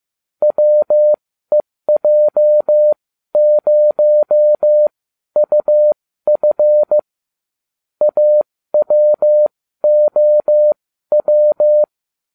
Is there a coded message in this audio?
Yes: Morse code from 0.4 to 11.8 s, heard as a keyed tone, about 15 wpm, spaced as for 12 wpm, 600 Hz. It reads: WEJ0UF AWOW